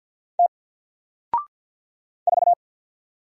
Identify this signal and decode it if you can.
Morse code, heard as a keyed tone, about 48 wpm, 700 Hz. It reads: T 4
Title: Single Letter-Number Alt 50wpm